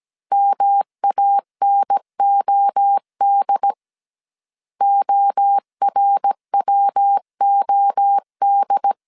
Morse code, heard as a keyed tone, about 17 wpm, 800 Hz. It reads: MANOD ORWOD